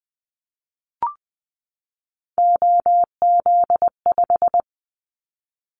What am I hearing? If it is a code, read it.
Morse code, heard as a keyed tone, about 20 wpm, 700 Hz. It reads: OZ5
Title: Callsign Prefixes for Top 50 DXCC - 20wpm